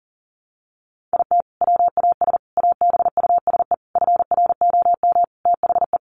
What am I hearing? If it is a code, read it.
Morse code, heard as a keyed tone, about 40 wpm, 700 Hz. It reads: IT WAS ABUSE FROM THE
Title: Sentences from Top 1300 Words 40wpm